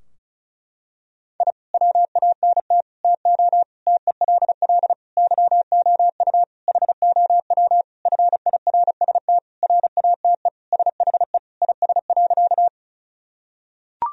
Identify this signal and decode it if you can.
Morse code, heard as a keyed tone, about 36 wpm, 700 Hz. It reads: I WANT TO TELL YOU HOW FIRST RATE SHE IS.
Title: Sentences from Top 1000 Words - Review Part 4 - 35wpm